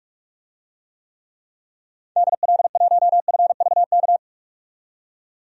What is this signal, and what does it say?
Morse code, heard as a keyed tone, about 46 wpm, 700 Hz. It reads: DL1FVK
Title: DXCC-100 Call Signs - 6 Characters - 45wpm